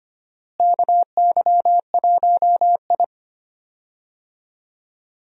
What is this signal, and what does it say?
Morse code, heard as a keyed tone, about 25 wpm, 700 Hz. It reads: KY1I
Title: USA Call Signs - 4 Characters - Repeated 2x - 25wpm